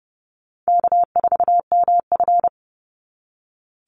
Morse code, heard as a keyed tone, about 28 wpm, 700 Hz. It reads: K4MF
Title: NA SOTA Chaser Callsigns 30wpm